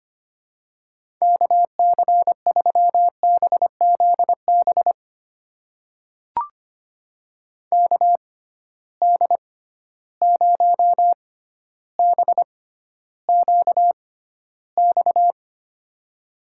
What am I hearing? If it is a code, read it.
Morse code, heard as a keyed tone, about 25 wpm, spaced as for 13 wpm, 700 Hz. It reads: KC3BZB KD0BQX
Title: USA Call Signs - 6 Characters - 6x Character Spacing - 25wpm